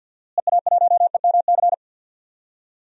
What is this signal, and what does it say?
Morse code, heard as a keyed tone, about 46 wpm, 700 Hz. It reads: EA1EMC
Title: DXCC-100 Call Signs - 6 Characters - Repeated 3x - 50wpm